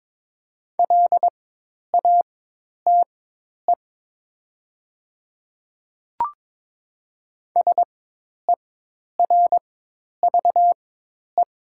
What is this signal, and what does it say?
Morse code, heard as a keyed tone, about 22 wpm, spaced as for 10 wpm, 700 Hz. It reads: LATE SERVE
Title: 4x Character Spacing - Top 500 Words - 22wpm